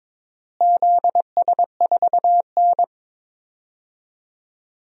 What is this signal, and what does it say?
Morse code, heard as a keyed tone, about 22 wpm, 700 Hz. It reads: ZS4N
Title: DXCC-100 Call Signs - 4 Characters - Repeated 3x - 22wpm